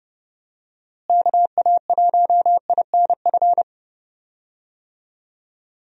Morse code, heard as a keyed tone, about 32 wpm, 700 Hz. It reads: KA1INF
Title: USA Call Signs - 6 Characters - Repeated 3x - 30wpm